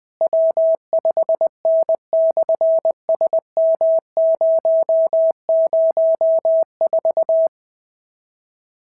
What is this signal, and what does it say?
Morse code, heard as a keyed tone, about 20 wpm, 650 Hz. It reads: W5N/SM004
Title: SOTA QSO Practice 20wpm